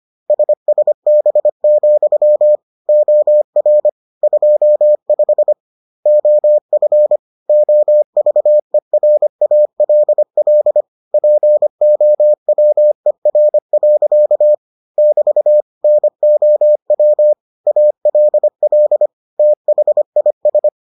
Morse code, heard as a keyed tone, about 25 wpm, 600 Hz. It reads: SSB, OR 25 OF OVERALL POWER. <BT> NOW ALL THIS